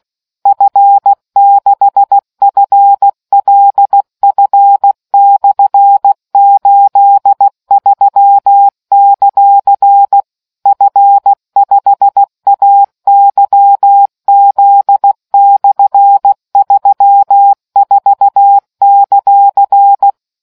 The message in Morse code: F6FLF/83; F5AYZ/34;

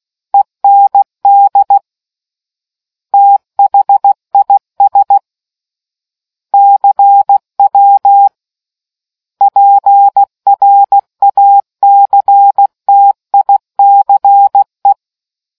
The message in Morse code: END THIS CW PRACTICE

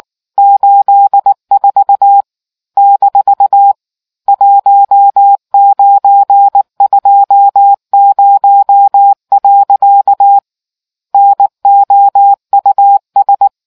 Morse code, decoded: 84 - 1920. NOUS

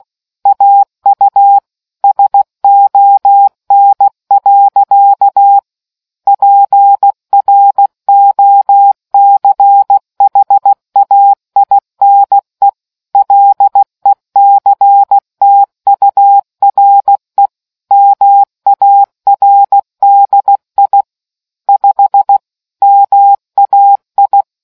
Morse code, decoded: AU SON. PROCHAINE LECTURE MARDI 5 MAI

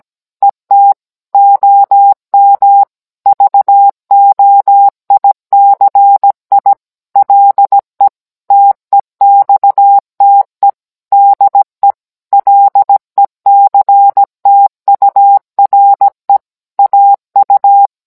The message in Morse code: ET OM VOICI LE TEXTE DE LECTURE AU

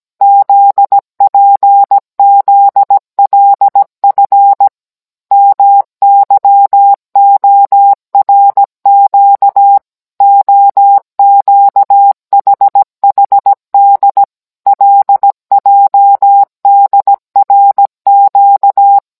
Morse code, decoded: ZPZLF MYORQ OQHHD LJDRQ

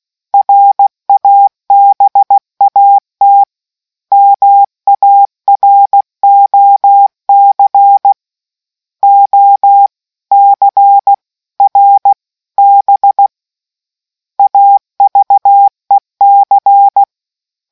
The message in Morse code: RABAT MAROC OCRB AVEC

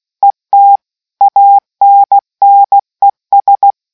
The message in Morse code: ET ANNES

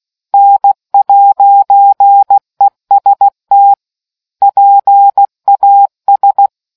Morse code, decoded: N'EST PAS

